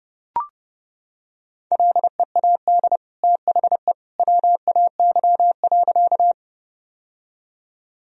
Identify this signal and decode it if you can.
Morse code, heard as a keyed tone, about 32 wpm, 700 Hz. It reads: LEAD THE WAY.
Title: Sentences from Top 1000 Words - Review Part 2 - 30wpm